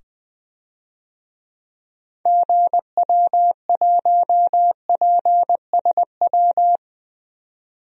Morse code, heard as a keyed tone, about 20 wpm, 700 Hz. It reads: GW1PSW
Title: DXCC-100 Call Signs - 6 Characters - 20wpm